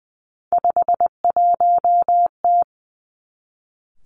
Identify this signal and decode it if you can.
Morse code, heard as a keyed tone, about 20 wpm, 700 Hz. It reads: 51T
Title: Three Challenging Letter-Number Combinations 20wpm